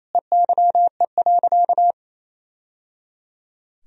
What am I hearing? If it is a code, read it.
Morse code, heard as a keyed tone, about 28 wpm, 700 Hz. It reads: EYE.
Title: Sentences from Top 1000 Words - Review Part 1 - 28wpm